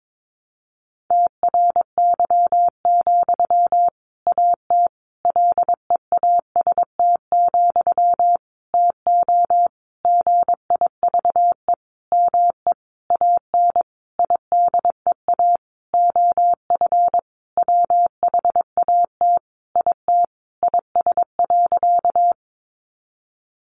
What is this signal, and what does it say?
Morse code, heard as a keyed tone, about 22 wpm, 700 Hz. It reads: TRY, AT LEAST, TO GIVE ME AN IDEA OF WHAT IT IS.
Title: Sentences from Top 700 Words 22wpm